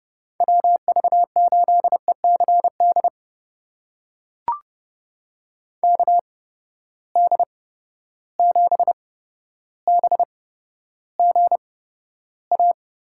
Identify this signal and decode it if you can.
Morse code, heard as a keyed tone, about 31 wpm, spaced as for 13 wpm, 700 Hz. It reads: WV8ECD KD7BGA
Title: USA Call Signs - 6 Characters - 8x Character Spacing - 30wpm